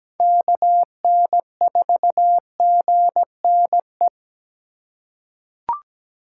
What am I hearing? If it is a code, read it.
Morse code, heard as a keyed tone, about 17 wpm, 700 Hz. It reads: KN4GNE
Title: USA Call Signs - 6 Characters - 2x Character Spacing - 17wpm